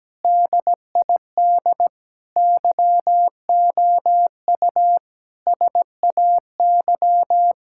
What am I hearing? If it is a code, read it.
Morse code, heard as a keyed tone, about 17 wpm, 700 Hz. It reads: DID YOU SAY
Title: Sentences from Top 1000 Words - Review Part 6 - 17wpm